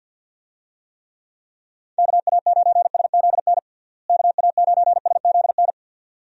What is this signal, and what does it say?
Morse code, heard as a keyed tone, about 53 wpm, 700 Hz. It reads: KA9SZN KA9SZN
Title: USA Call Signs - 6 Characters - Repeated 2x - 50wpm